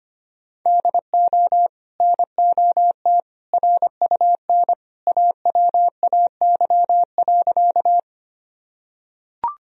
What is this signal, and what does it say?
Morse code, heard as a keyed tone, about 25 wpm, 700 Hz. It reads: DO NOT RUN AWAY.
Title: Sentences from Top 1000 Words - Review Part 1 - 25wpm